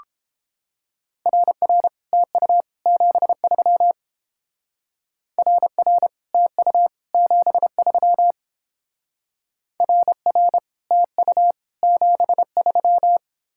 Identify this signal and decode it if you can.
Morse code, heard as a keyed tone, about 31 wpm, 700 Hz. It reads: RR TU 73 RR TU 73 RR TU 73